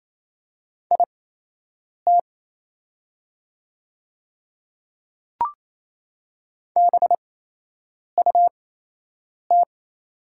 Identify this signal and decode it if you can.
Morse code, heard as a keyed tone, about 27 wpm, spaced as for 8 wpm, 700 Hz. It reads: IT BUT
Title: 8x Character Spacing - Top 100 Words - 28wpm